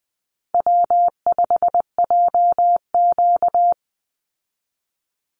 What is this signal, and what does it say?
Morse code, heard as a keyed tone, about 20 wpm, 700 Hz. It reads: W5JQ
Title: Call Signs Repeated 3x - 20wpm